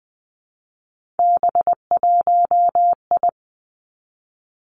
Morse code, heard as a keyed tone, about 20 wpm, 700 Hz. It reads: B1I
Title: Three Challenging Letter-Number Combinations 20wpm